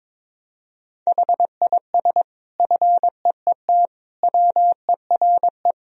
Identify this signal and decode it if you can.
Morse code, heard as a keyed tone, about 22 wpm, 700 Hz. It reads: HIS FEET WERE
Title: Sentences from Top 1000 Words - Review Part 1 - 22wpm